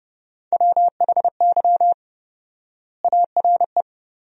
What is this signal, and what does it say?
Morse code, heard as a keyed tone, about 30 wpm, 700 Hz. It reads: WHY ARE
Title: 4x Word-Spacing - Sentences from Top 200 Words - 30wpm